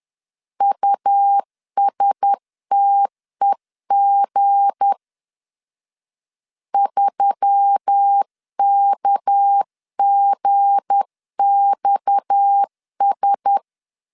Morse code, heard as a keyed tone, about 10 wpm, 800 Hz. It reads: USTEG 3KGXS
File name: LEZIONI / 7 GF9Z / 7 G 10wpm 2501